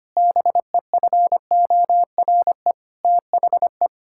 Morse code, heard as a keyed tone, about 25 wpm, 700 Hz. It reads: BEFORE THE
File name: practice / 25wpm / Sentences with 9-Word Phrases - 25wpm